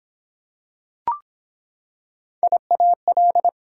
Morse code, heard as a keyed tone, about 23 wpm, 700 Hz. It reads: IAL